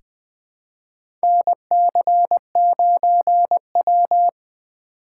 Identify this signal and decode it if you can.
Morse code, heard as a keyed tone, about 20 wpm, 700 Hz. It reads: NC9W
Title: K1USN SST QSO Practice - 20wpm